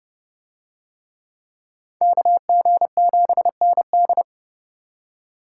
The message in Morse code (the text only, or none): KG7ND